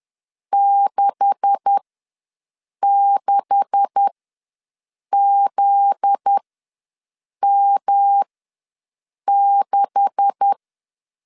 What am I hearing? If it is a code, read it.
Morse code, heard as a keyed tone, about 10 wpm, spaced as for 6 wpm, 800 Hz. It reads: 66ZM6
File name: LEZIONI / 9 286 ws 10wpm 4032